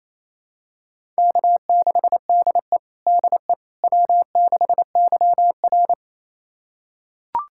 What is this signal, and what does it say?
Morse code, heard as a keyed tone, about 28 wpm, 700 Hz. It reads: K6DE DE W6YR
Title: USA Callsign DE Callsign - 4 Characters - 28wpm